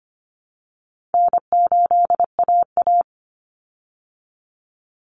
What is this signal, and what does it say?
Morse code, heard as a keyed tone, about 25 wpm, 700 Hz. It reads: N8AA
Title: Call Signs Repeated 2x - 25wpm